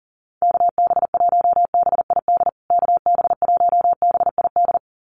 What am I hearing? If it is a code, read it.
Morse code, heard as a keyed tone, about 38 wpm, 700 Hz. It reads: KB1BID KB1BID
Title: USA Call Signs Repeated 2x 40wpm